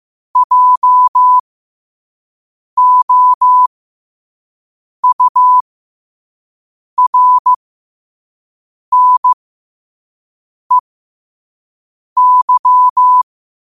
Morse code, heard as a keyed tone, about 15 wpm, spaced as for 5 wpm, 1000 Hz. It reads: JOURNEY